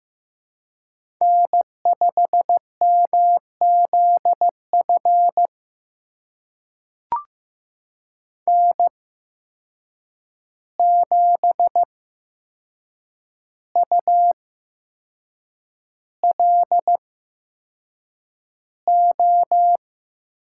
Morse code, heard as a keyed tone, about 15 wpm, spaced as for 6 wpm, 700 Hz. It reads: N5MZF N7ULO
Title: USA Call Signs - 5 Characters - 8x Character Spacing - 15wpm